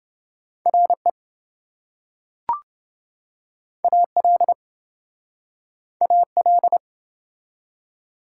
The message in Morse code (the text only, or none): RE AL AL